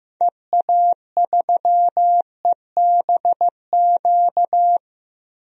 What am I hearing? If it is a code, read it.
Morse code, heard as a keyed tone, about 15 wpm, 700 Hz. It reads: EA3EBQ
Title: DXCC-100 Call Signs - 6 Characters - 15wpm